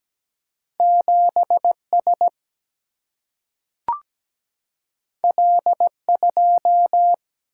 Morse code, heard as a keyed tone, about 17 wpm, 700 Hz. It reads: 7S L2
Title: Single-Letter Single-Number 17wpm